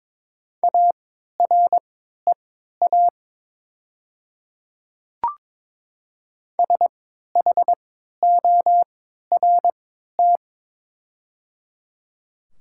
Morse code, heard as a keyed tone, about 23 wpm, spaced as for 13 wpm, 700 Hz. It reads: AREA SHORT
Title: 3x Character Spacing - Top 500 Words - 22wpm